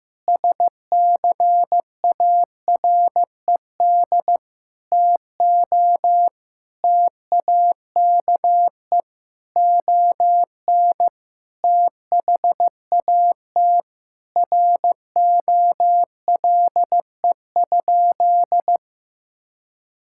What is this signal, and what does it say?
Morse code, heard as a keyed tone, about 15 wpm, 700 Hz. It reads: SCARED TO TAKE ON THAT ROLE?
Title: Sentences from Top 1000 Words - Review Part 6 - 15wpm